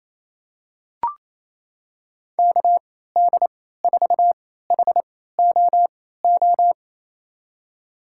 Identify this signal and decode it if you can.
Morse code, heard as a keyed tone, about 28 wpm, spaced as for 16 wpm, 700 Hz. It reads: KD4HOO
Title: USA Call Signs - 6 Characters - 3x Character Spacing - 28wpm